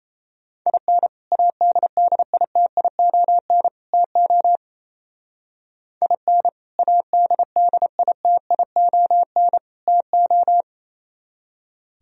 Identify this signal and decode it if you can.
Morse code, heard as a keyed tone, about 33 wpm, 700 Hz. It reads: IN ADDITION TO IN ADDITION TO
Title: Speed-Racing - Sets of 3 Words 28wpm